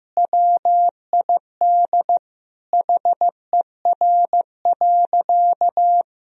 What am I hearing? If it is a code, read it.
Morse code, heard as a keyed tone, about 15 wpm, 700 Hz. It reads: WID HER.
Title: Sentences from Top 100 Words and Abbreviations 15wpm